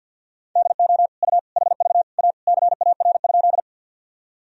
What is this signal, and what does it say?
Morse code, heard as a keyed tone, about 50 wpm, 700 Hz. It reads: DO U HV A CAR?
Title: Sentences from Top 300 Words and Abbreviations 50wpm